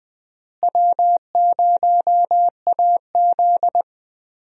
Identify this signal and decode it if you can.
Morse code, heard as a keyed tone, about 20 wpm, 700 Hz. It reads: W0AZ
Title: USA Call Signs - 4 Characters - Repeated 3x - 20wpm